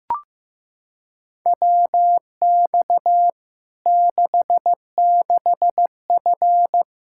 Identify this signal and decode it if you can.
Morse code, heard as a keyed tone, about 15 wpm, 700 Hz. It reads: WX 66F
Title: Practice QSO - Exchange 2 - Info - 15wpm